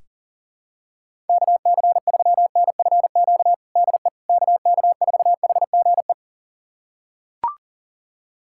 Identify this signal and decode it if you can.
Morse code, heard as a keyed tone, about 39 wpm, 700 Hz. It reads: KC3NFQ DE KK4HGE
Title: USA Callsign DE Callsign - 6 Characters - 40wpm